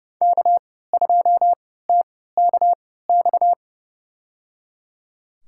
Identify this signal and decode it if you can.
Morse code, heard as a keyed tone, about 30 wpm, spaced as for 17 wpm, 700 Hz. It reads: K2TKX
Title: USA Call Signs - 5 Characters - 3x Character Spacing - 30wpm